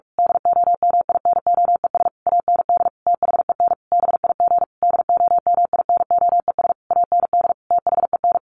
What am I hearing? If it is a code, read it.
Morse code, heard as a keyed tone, about 44 wpm, 700 Hz. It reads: DOMINOES AND THEN BIG DOMINOES AND THEN
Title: A Sound Of Thunder 45wpm